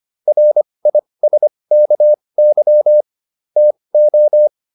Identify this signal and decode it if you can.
Morse code, heard as a keyed tone, about 25 wpm, spaced as for 20 wpm, 600 Hz. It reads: RISKY TO